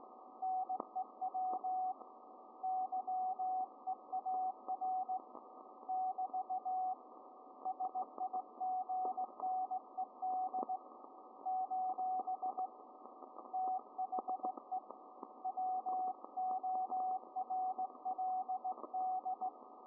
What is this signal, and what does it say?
Morse code, heard as a keyed tone, about 18 wpm, 750 Hz, with radio noise. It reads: NEW YEAR <BT> SIGNED : THE WORLD